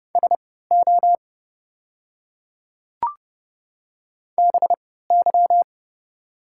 Morse code, heard as a keyed tone, about 30 wpm, spaced as for 17 wpm, 700 Hz. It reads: SO BY
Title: 3x Character Spacing - Top 200 Words - 30wpm